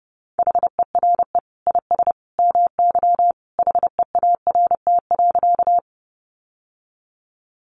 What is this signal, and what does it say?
Morse code, heard as a keyed tone, about 28 wpm, 700 Hz. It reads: HERE IS MY HEART.
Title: Sentences from Top 1200 Words 30wpm